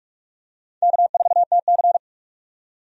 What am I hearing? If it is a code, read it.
Morse code, heard as a keyed tone, about 46 wpm, 700 Hz. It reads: K4TC
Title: USA Call Signs - 4 Characters - 45wpm